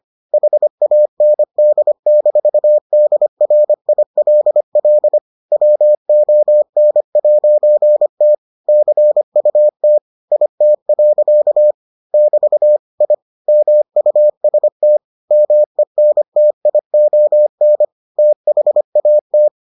Morse code, heard as a keyed tone, about 25 wpm, 600 Hz. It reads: HAND-DRILL WON'T CUT IT. <BT> I MUST MENTION THAT